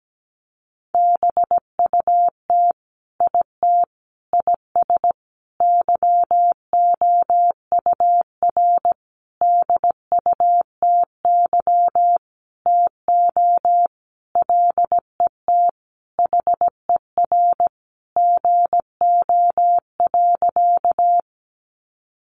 Morse code, heard as a keyed tone, about 17 wpm, 700 Hz. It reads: BUT IT IS YOUR DUTY TO LET HER GO.
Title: Sentences from Top 1500 Words 17wpm